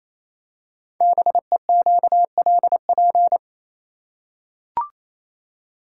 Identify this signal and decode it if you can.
Morse code, heard as a keyed tone, about 27 wpm, 700 Hz. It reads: BEQLP
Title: Code Groups - 5 Letters - 28wpm